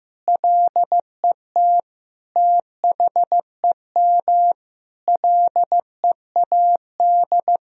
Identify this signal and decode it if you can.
Morse code, heard as a keyed tone, about 15 wpm, 700 Hz. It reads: LET THEM LEAD
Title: Sentences from Top 1000 Words - Review Part 2 - 15wpm